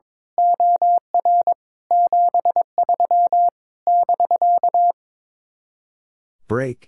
Morse code, heard as a keyed tone, about 22 wpm, 700 Hz. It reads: OR 73 <BK>
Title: Speed-Racing Common QSO Phrases Encore - 22wpm